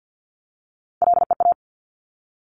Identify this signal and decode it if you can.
Morse code, heard as a keyed tone, about 50 wpm, 700 Hz. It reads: LEA
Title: Top Three Letter Combinations in Words 50wpm